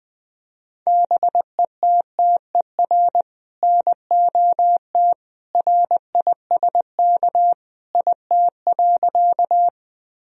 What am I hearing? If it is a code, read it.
Morse code, heard as a keyed tone, about 20 wpm, 700 Hz. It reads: BETTER NOT RISK IT.
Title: Sentences from Top 1000 Words - Review Part 4 - 20wpm